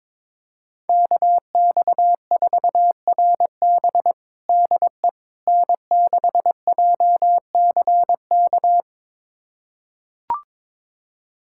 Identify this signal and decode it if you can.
Morse code, heard as a keyed tone, about 22 wpm, 700 Hz. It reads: KX4RB DE N6JCK
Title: USA Callsign DE Callsign - 5 Characters - 22wpm